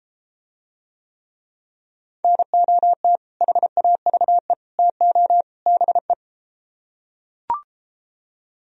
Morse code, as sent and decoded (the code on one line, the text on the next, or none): -. --- - / .... .- ...- . / - --- / -... .
NOT HAVE TO BE